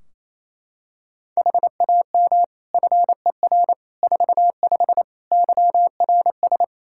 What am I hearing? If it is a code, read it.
Morse code, heard as a keyed tone, about 28 wpm, 700 Hz. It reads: HAM FER 45 YRS
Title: Practice QSO - Exchange 3 - More Info - 28wpm